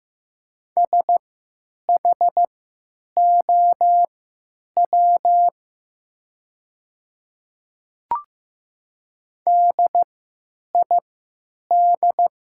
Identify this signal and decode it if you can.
Morse code, heard as a keyed tone, about 15 wpm, spaced as for 9 wpm, 700 Hz. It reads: SHOW DID
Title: 3x Character Spacing - Top 300 Words - 15wpm